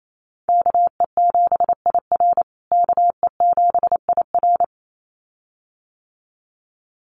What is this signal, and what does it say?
Morse code, heard as a keyed tone, about 28 wpm, 700 Hz. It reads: KE7IR KE7IR
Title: USA Call Signs Repeated 2x 28wpm